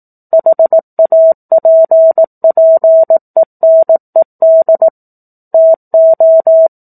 Morse code, heard as a keyed tone, about 18 wpm, 650 Hz. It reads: HAPPENED TO
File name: 04-Chapter-04 18 WPM